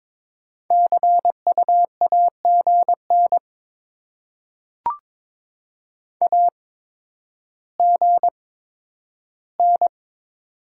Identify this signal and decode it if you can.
Morse code, heard as a keyed tone, about 22 wpm, spaced as for 11 wpm, 700 Hz. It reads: CU AGN AGN
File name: Top 20 QSO Elements - 8x Character Spacing - 22wpm